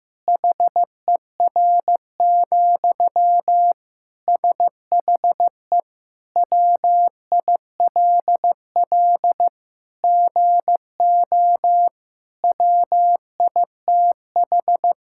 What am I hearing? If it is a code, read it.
Morse code, heard as a keyed tone, about 15 wpm, 700 Hz. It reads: HER, SHE WILL GO WITH
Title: Sentences from Top 1600 Words 15wpm